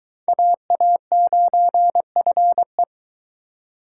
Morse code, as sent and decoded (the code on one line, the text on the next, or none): .- .- ----. ..-. .
AA9FE